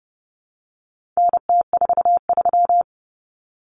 Morse code, heard as a keyed tone, about 30 wpm, 700 Hz. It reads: NT43
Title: Common Cut Numbers - 4-Digit 30wpm